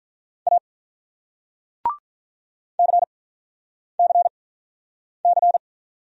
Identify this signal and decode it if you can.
Morse code, heard as a keyed tone, about 44 wpm, spaced as for 9 wpm, 700 Hz. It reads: A CCC